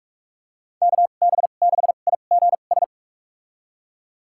Morse code, heard as a keyed tone, about 45 wpm, spaced as for 32 wpm, 700 Hz. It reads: KB6IGS
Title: USA Call Signs - 2x Character Spacing - 45wpm